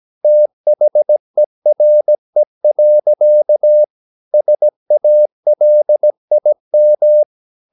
Morse code, heard as a keyed tone, about 17 wpm, 600 Hz. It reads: THERE. SALIM
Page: Romanen Never af Ken Follet på engelsk.